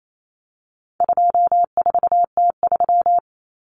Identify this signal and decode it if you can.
Morse code, heard as a keyed tone, about 28 wpm, 700 Hz. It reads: 24T3
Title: Common Cut Numbers - 4-Digit 28wpm